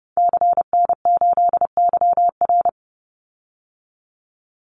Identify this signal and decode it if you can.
Morse code, heard as a keyed tone, about 30 wpm, 700 Hz. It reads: CN8YR
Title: Call Signs Repeated 3x - 30wpm